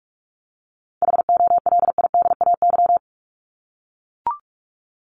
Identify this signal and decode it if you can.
Morse code, heard as a keyed tone, about 44 wpm, 700 Hz. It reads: HOLIDAY
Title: Top 1501-1750 Words 45wpm